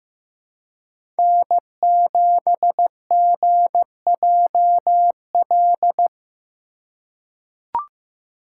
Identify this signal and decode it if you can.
Morse code, heard as a keyed tone, about 15 wpm, 700 Hz. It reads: N7GJL
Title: USA Call Signs - 5 Characters - 4x Character Spacing - 15wpm